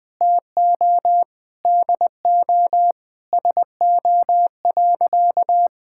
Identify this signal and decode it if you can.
Morse code, heard as a keyed tone, about 20 wpm, 700 Hz. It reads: TO DO SO.
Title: Sentences with 3-Word Phrases - Spoken Beginning - 20wpm